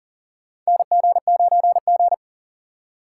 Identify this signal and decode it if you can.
Morse code, heard as a keyed tone, about 40 wpm, 700 Hz. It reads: NG9G
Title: USA Call Signs - 4 Characters - 3x Character Spacing - 40wpm